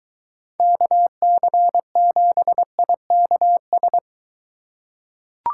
Morse code, heard as a keyed tone, about 23 wpm, 700 Hz. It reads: KC7IKS